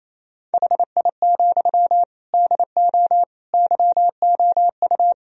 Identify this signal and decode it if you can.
Morse code, heard as a keyed tone, about 28 wpm, 700 Hz. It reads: HI, DO YOU
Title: Sentences from Top 1000 Words - Review Part 3 - 28wpm